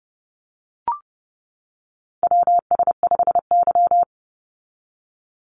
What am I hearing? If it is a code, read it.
Morse code, heard as a keyed tone, about 30 wpm, 700 Hz. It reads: WS5Y
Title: USA Call Signs 30wpm